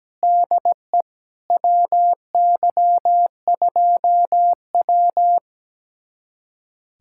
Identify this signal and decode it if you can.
Morse code, heard as a keyed tone, about 17 wpm, 700 Hz. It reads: DE WY2W
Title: USA Callsign DE Callsign - 4 Characters - 17wpm